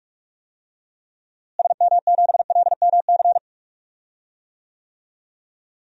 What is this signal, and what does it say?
Morse code, heard as a keyed tone, about 46 wpm, 700 Hz. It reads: SM7LMC
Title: DXCC-100 Call Signs - 6 Characters - Repeated 2x - 45wpm